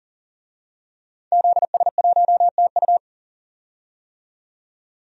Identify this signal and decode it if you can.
Morse code, heard as a keyed tone, about 38 wpm, 700 Hz. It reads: ZS1TU
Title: DXCC-100 Call Signs - 5 Characters - 40wpm